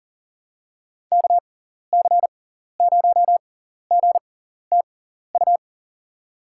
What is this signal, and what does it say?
Morse code, heard as a keyed tone, about 37 wpm, spaced as for 14 wpm, 700 Hz. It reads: KC0GTU
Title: USA Call Signs - 6x Character Spacing - 40wpm